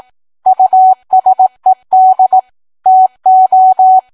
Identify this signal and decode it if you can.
Morse code, heard as a keyed tone, about 18 wpm, 750 Hz. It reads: USED TO